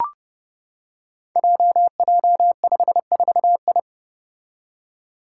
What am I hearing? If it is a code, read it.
Morse code, heard as a keyed tone, about 30 wpm, 700 Hz. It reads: JJ54I
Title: Five Challenging Letter-Number Combinations 30wpm